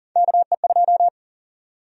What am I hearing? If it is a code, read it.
Morse code, heard as a keyed tone, about 40 wpm, 700 Hz. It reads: KE2
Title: USA Call Sign Prefixes - 40wpm